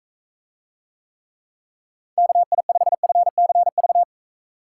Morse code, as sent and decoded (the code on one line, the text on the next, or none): -.- .. ..... ..-. -.-. ...-
KI5FCV